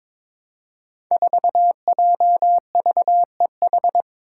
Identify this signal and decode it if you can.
Morse code, heard as a keyed tone, about 22 wpm, 700 Hz. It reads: 4JVEH
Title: Five Challenging Letter-Number Combinations 22wpm